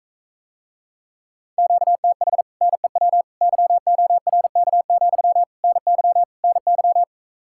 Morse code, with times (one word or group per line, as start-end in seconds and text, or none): 1.6-2.4 QTH
2.6-3.2 NEW
3.4-5.4 YORK,
5.6-6.2 NY
6.4-7.0 NY